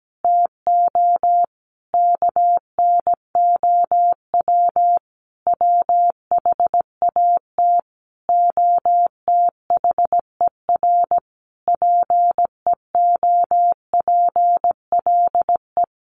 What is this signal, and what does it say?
Morse code, heard as a keyed tone, about 17 wpm, 700 Hz. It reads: TO KNOW WHAT OTHER PEOPLE
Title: Sentences from Top 600 Words 17wpm